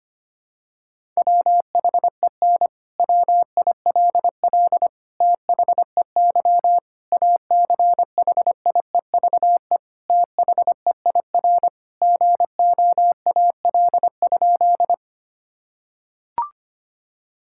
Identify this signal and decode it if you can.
Morse code, heard as a keyed tone, about 25 wpm, 700 Hz. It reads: WHEN WILL THEY ACHIEVE THEIR GOAL?
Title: Sentences from Top 1000 Words - Review Part 6 - 25wpm